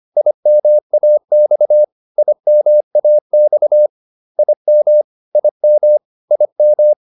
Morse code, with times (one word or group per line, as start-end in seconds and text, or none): 0.2-1.8 IMAX
2.2-3.9 IMAX
4.4-5.0 IM
5.3-6.0 IM
6.3-6.9 IM